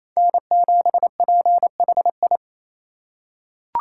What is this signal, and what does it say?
Morse code, heard as a keyed tone, about 28 wpm, 700 Hz. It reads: N7PHI
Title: K1USN SST QSO Practice - 28wpm